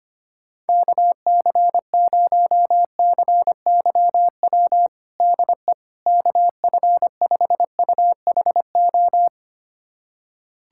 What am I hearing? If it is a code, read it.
Morse code, heard as a keyed tone, about 25 wpm, 700 Hz. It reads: KC0CYW DE KF5UHO
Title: USA Callsign DE Callsign - 6 Characters - 25wpm